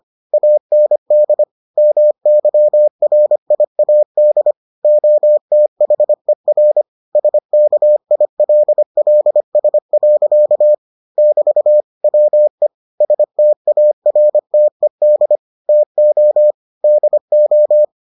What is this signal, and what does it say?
Morse code, heard as a keyed tone, about 25 wpm, 600 Hz. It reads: AND MYRIAD OTHER SKILLS. <BT> WE STARTED TO DO